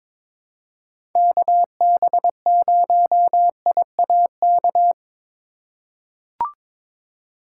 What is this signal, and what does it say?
Morse code, heard as a keyed tone, about 22 wpm, 700 Hz. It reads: KB0IAK
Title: USA Call Signs - 6 Characters - 3x Character Spacing - 22wpm